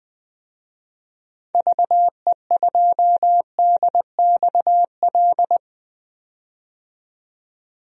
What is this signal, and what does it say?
Morse code, heard as a keyed tone, about 20 wpm, 700 Hz. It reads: VE2DXL
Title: DXCC-100 Call Signs - 6 Characters - 20wpm